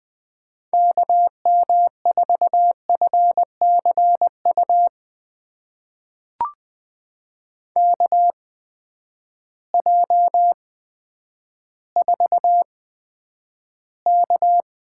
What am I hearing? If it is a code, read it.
Morse code, heard as a keyed tone, about 20 wpm, spaced as for 10 wpm, 700 Hz. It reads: KM4FCU KJ4K
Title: USA Call Signs - 6 Characters - 8x Character Spacing - 20wpm